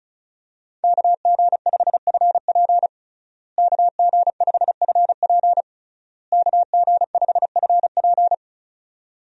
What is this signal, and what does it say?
Morse code, heard as a keyed tone, about 36 wpm, 700 Hz. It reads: KG5FP KG5FP KG5FP
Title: USA Call Signs - 5 Characters - Repeated 3x - 35wpm